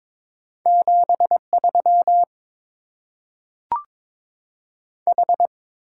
Morse code, heard as a keyed tone, about 22 wpm, 700 Hz. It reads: 7 3 H
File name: Top 20 QSO Elements - 8x Character Spacing - 22wpm